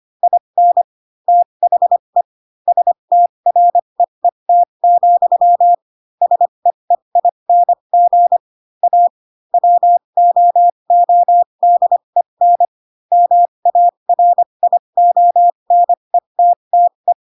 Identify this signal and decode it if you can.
Morse code, heard as a keyed tone, about 25 wpm, spaced as for 22 wpm, 700 Hz. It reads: IN THE STREET, SEEING A WOODEN MARIONETTE
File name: Pinocchio / Pinocchio ch3 18 25 ik1hge